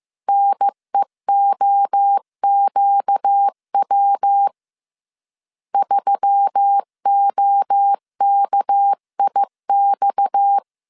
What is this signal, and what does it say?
Morse code, heard as a keyed tone, about 15 wpm, 800 Hz. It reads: NEOQW 3OKIX
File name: CW / DET / 1 6 1500C S C 14wpm 2951